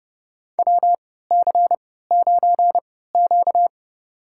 Morse code, heard as a keyed tone, about 32 wpm, spaced as for 17 wpm, 700 Hz. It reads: WC9Q
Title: USA Call Signs - 4 Characters - 3x Character Spacing - 30wpm